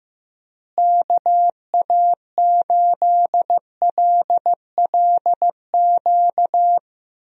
Morse code, heard as a keyed tone, about 15 wpm, 700 Hz. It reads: KA8LLQ
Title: USA Call Signs - 6 Characters - Repeated 3x - 15wpm